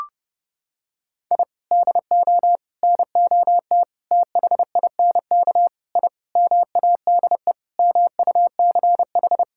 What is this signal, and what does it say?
Morse code, heard as a keyed tone, about 30 wpm, 700 Hz. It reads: I DO NOT THINK I MADE MUCH
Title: Sentences from Top 1000 Words - Review Part 6 - 30wpm